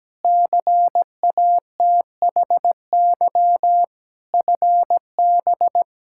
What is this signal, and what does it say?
Morse code, heard as a keyed tone, about 17 wpm, 700 Hz. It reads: CATHY FB
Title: Practice QSO - Exchange 2 - Info - 17wpm